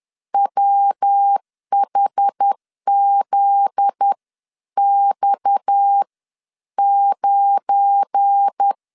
Morse code, heard as a keyed tone, about 10 wpm, spaced as for 9 wpm, 800 Hz. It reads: WHZX9
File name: CW / DET / 1 9 1500C S B 10wpm 4439